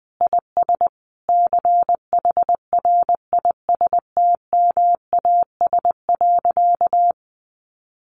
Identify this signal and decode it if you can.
Morse code, heard as a keyed tone, about 20 wpm, 700 Hz. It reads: IS CHRISTMAS.
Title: Easy Phrases 20wpm